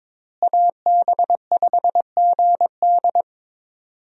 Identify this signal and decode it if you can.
Morse code, heard as a keyed tone, about 22 wpm, 700 Hz. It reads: AB5GD
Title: USA Call Signs - 5 Characters - Repeated 2x - 22wpm